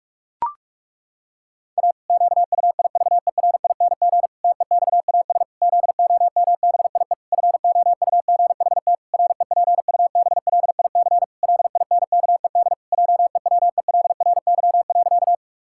Morse code, heard as a keyed tone, about 46 wpm, 700 Hz. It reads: A QUIVERING TEXAS ZOMBIE FOUGHT REPUBLIC LINKED JEWELRY.
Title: Pangram Challenge v2 - 45wpm